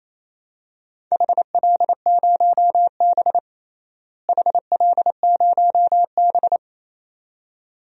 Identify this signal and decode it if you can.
Morse code, heard as a keyed tone, about 28 wpm, 700 Hz. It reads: HL0B HL0B
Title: DXCC-100 Call Signs - 4 Characters - Repeated 2x - 28wpm